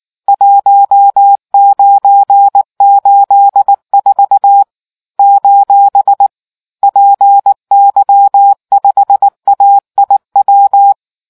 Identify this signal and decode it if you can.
Morse code, heard as a keyed tone, about 19 wpm, 800 Hz. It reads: 1984 : PY5AIW